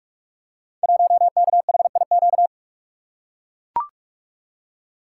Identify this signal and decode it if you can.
Morse code, heard as a keyed tone, about 44 wpm, 690 Hz. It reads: 1KHIQ
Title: Code Groups - 5 Letter:Numbers - 45wpm